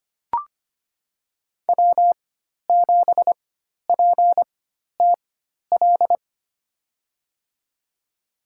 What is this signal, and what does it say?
Morse code, heard as a keyed tone, about 25 wpm, spaced as for 12 wpm, 700 Hz. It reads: W7PTL
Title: USA Call Signs - 5 Characters - 4x Character Spacing - 25wpm